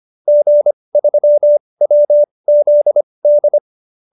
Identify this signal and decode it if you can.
Morse code, heard as a keyed tone, about 25 wpm, spaced as for 20 wpm, 590 Hz. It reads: G3WZD
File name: QSO310 20